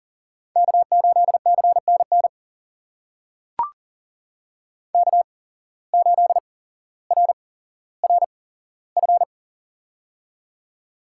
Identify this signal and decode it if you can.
Morse code, heard as a keyed tone, about 39 wpm, spaced as for 17 wpm, 700 Hz. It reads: K8CNN K8RRF
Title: USA Call Signs - 8x Character Spacing - 40wpm